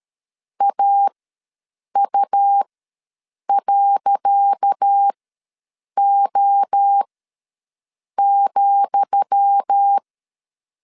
Morse code, heard as a keyed tone, about 13 wpm, spaced as for 7 wpm, 800 Hz. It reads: AU.O,
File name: LEZIONI / 10 SIGNS / 10D COMA ws 12wpm 3618